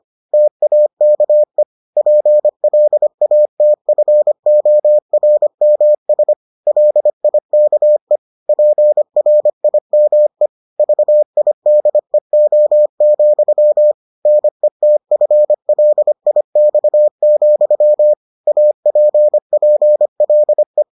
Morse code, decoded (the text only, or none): TAKE PLATFORMS LIKE PRIME VIDEO, NETFLIX, APPLE